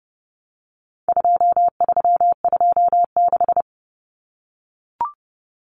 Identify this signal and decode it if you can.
Morse code, heard as a keyed tone, about 30 wpm, 700 Hz. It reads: 2326
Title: Common Cut Numbers - 4-Digit 30wpm